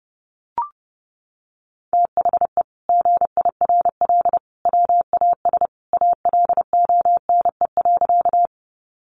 Morse code, heard as a keyed tone, about 30 wpm, 700 Hz. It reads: THE GIRL WAS ALONE.
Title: Sentences from Top 1000 Words 30wpm